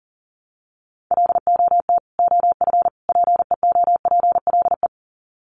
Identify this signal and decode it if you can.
Morse code, heard as a keyed tone, about 40 wpm, 700 Hz. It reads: LOT OF PEOPLE
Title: Sets of 3 Words 40wpm